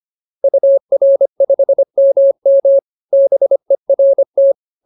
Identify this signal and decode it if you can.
Morse code, heard as a keyed tone, about 25 wpm, 550 Hz. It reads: UR5MM BERT